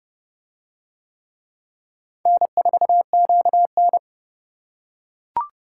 Morse code, heard as a keyed tone, about 29 wpm, 700 Hz. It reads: N4QN